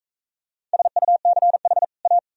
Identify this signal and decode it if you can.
Morse code, heard as a keyed tone, about 43 wpm, 700 Hz. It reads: SUCH A